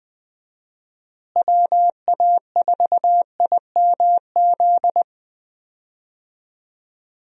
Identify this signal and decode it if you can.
Morse code, heard as a keyed tone, about 20 wpm, 700 Hz. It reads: WA4IMZ
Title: USA Call Signs - 6 Characters - Repeated 3x - 20wpm